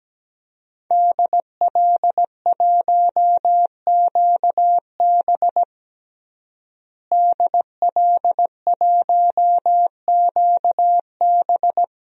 Morse code, heard as a keyed tone, about 17 wpm, 700 Hz. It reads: DL1QB DL1QB
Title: DXCC-100 Call Signs - 5 Characters - Repeated 2x - 17wpm